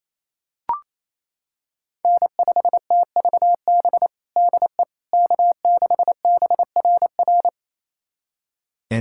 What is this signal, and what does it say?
Morse code, heard as a keyed tone, about 28 wpm, 700 Hz. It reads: N5TVB DE K6BRR
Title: USA Callsign DE Callsign - 5 Characters - 28wpm